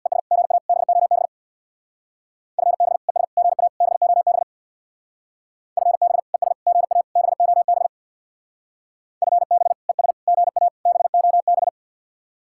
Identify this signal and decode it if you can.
Morse code, heard as a keyed tone, about 60 wpm, 700 Hz. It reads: ES GA BOB FB ES GA BOB FB ES GA BOB FB ES GA BOB
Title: Speed-Racing Common QSO Phrases - 50wpm